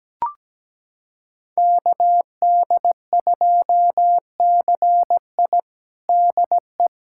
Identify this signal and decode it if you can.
Morse code, heard as a keyed tone, about 17 wpm, 700 Hz. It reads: KD2CI DE
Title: USA Callsign DE Callsign - 5 Characters - 17wpm